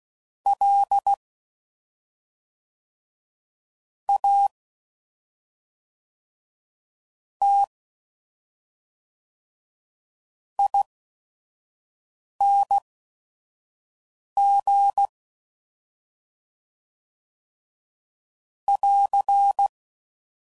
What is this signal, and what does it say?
Morse code, heard as a keyed tone, about 16 wpm, spaced as for 5 wpm, 800 Hz. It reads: L A T ING <AR>
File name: Suffixes / Special ING's / SlowToFaster ING's